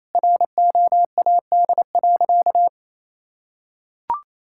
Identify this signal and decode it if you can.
Morse code, heard as a keyed tone, about 28 wpm, 700 Hz. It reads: ROAD.
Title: Sentences from Top 1600 Words 28wpm